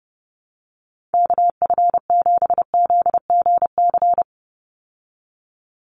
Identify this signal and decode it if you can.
Morse code, heard as a keyed tone, about 30 wpm, 700 Hz. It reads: KF7ZGC
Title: USA Call Signs 30wpm